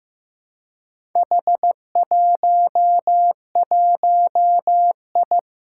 Morse code, heard as a keyed tone, about 15 wpm, 700 Hz. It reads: H11I
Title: Four Challenging Letter-Number Combinations 15wpm